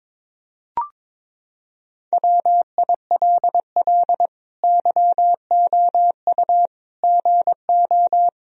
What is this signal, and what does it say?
Morse code, heard as a keyed tone, about 22 wpm, 700 Hz. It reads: WILL YOU GO
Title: Sentences from Top 1000 Words - Review Part 2 - 22wpm